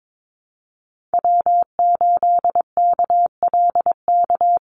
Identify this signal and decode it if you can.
Morse code, heard as a keyed tone, about 22 wpm, 700 Hz. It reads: W8KLK
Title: USA Call Signs 22wpm